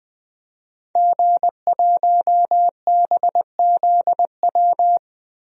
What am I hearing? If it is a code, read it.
Morse code, heard as a keyed tone, about 20 wpm, 700 Hz. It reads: G1BZW
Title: DXCC-100 Call Signs - 5 Characters - Repeated 3x - 20wpm